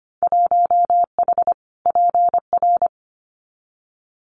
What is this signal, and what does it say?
Morse code, heard as a keyed tone, about 25 wpm, 700 Hz. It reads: 1H PR
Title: Winter Field Day Exchanges Encore 25wpm